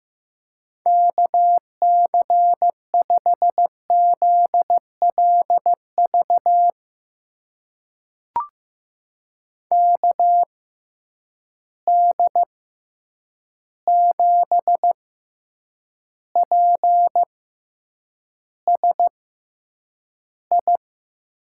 Morse code, heard as a keyed tone, about 15 wpm, spaced as for 8 wpm, 700 Hz. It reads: KC5ZLV KD7PSI
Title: USA Call Signs - 6 Characters - 6x Character Spacing - 15wpm